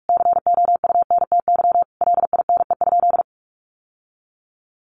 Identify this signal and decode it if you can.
Morse code, heard as a keyed tone, about 44 wpm, 700 Hz. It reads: COUNTY LINE?
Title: Sentences from Top 1200 Words 45wpm